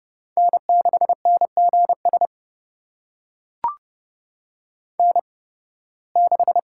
Morse code, heard as a keyed tone, about 30 wpm, spaced as for 20 wpm, 700 Hz. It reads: N6NGS N6
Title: USA Call Signs - 5 Characters - 8x Character Spacing - 30wpm